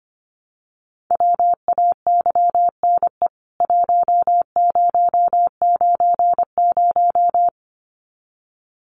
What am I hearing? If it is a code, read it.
Morse code, heard as a keyed tone, about 25 wpm, 700 Hz. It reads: WAYNE 1090
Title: CWT Exchange 25wpm